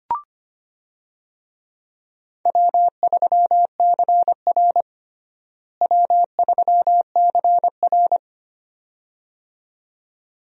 Morse code, heard as a keyed tone, about 25 wpm, 700 Hz. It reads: W3CR W3CR
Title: USA Call Signs - 4 Characters - Repeated 2x - 25wpm